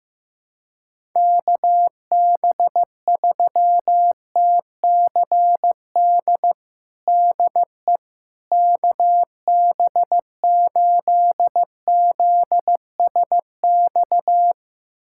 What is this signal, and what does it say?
Morse code, heard as a keyed tone, about 15 wpm, 700 Hz. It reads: KB3TCD DE KB8ZSX
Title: USA Callsign DE Callsign - 6 Characters - 15wpm